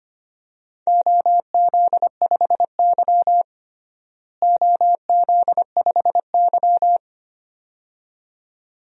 Morse code, heard as a keyed tone, about 25 wpm, 700 Hz. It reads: OZ5Y OZ5Y
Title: DXCC-100 Call Signs - 4 Characters - Repeated 2x - 25wpm